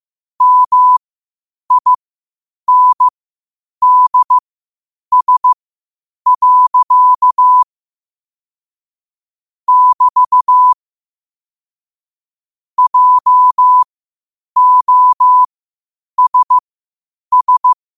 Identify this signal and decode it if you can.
Morse code, heard as a keyed tone, about 15 wpm, spaced as for 8 wpm, 1000 Hz. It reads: MINDS. <BT> JOSS
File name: qotdfiles / 8wpm